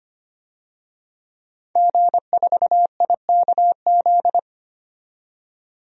Morse code, heard as a keyed tone, about 25 wpm, 700 Hz. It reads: G4IKZ
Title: DXCC-100 Call Signs - 5 Characters - 25wpm